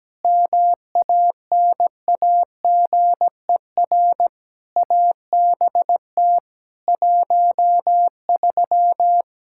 Morse code, heard as a keyed tone, about 17 wpm, 700 Hz. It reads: MANAGER ABT 13
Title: K1USN SST QSO Practice 17wpm